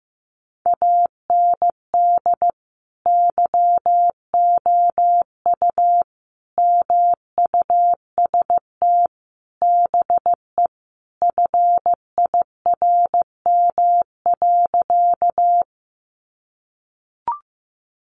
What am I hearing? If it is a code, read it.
Morse code, heard as a keyed tone, about 14 wpm, 700 Hz. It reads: AND YOU MUST BE FIRM.
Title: Sentences from Top 1400 Words 15wpm